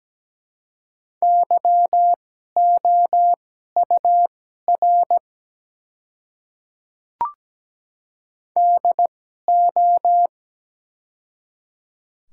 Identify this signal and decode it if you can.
Morse code, heard as a keyed tone, about 17 wpm, spaced as for 12 wpm, 690 Hz. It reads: YOUR DO
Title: 2x Character Spacing - Top 100 Words - 17wpm